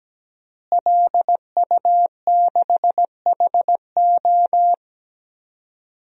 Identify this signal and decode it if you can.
Morse code, heard as a keyed tone, about 17 wpm, 700 Hz. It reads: LU6HO
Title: DXCC-100 Call Signs - 5 Characters - Repeated 3x - 17wpm